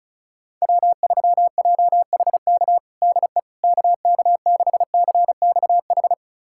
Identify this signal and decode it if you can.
Morse code, heard as a keyed tone, about 36 wpm, 700 Hz. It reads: W3JHK DE KK6CXH
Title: USA Callsign DE Callsign 35wpm